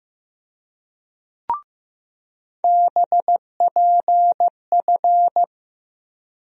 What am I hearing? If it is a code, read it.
Morse code, heard as a keyed tone, about 15 wpm, 700 Hz. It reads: BPF
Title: USA Call Sign Postfixes - 3 Letters 15wpm